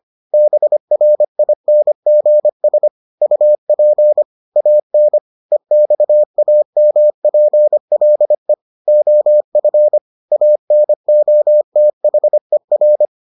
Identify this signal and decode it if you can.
Morse code, heard as a keyed tone, about 25 wpm, 600 Hz. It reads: BRINGS UP AN EXAMPLE OF ANOTHER